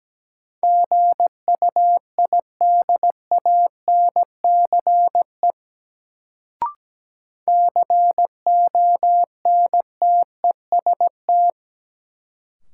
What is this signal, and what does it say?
Morse code, heard as a keyed tone, about 17 wpm, 700 Hz. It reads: GUIDANCE CONTEST
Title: Sentences from Top 1600 Words - New Word Drill - 17wpm